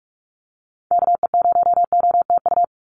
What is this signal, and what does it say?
Morse code, heard as a keyed tone, about 46 wpm, 690 Hz. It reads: KE0OTU
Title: USA Call Signs 45wpm